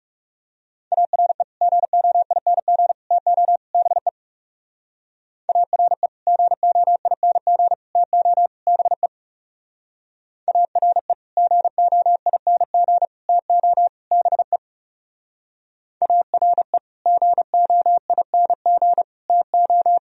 Morse code, heard as a keyed tone, about 35 wpm, 700 Hz. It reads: ARE GOING TO BE ARE GOING TO BE ARE GOING TO BE ARE GOING TO
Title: Speed-Racing - Sets of 4 Words 30wpm